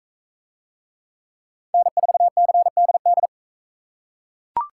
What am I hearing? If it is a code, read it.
Morse code, heard as a keyed tone, about 40 wpm, 700 Hz. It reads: N4CDD